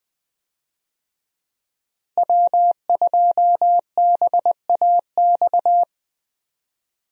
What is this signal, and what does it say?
Morse code, heard as a keyed tone, about 20 wpm, 700 Hz. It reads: W2BAX
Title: USA Call Signs - 5 Characters - Repeated 2x - 20wpm